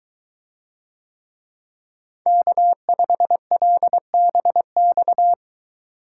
Morse code, heard as a keyed tone, about 23 wpm, 700 Hz. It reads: K5LBX